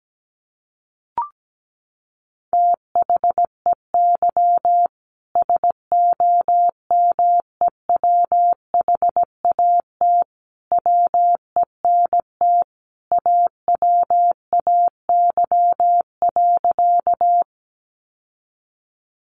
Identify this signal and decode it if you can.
Morse code, heard as a keyed tone, about 17 wpm, 700 Hz. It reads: THEY SOMEWHAT WENT AWAY.
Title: Sentences from Top 1300 Words 17wpm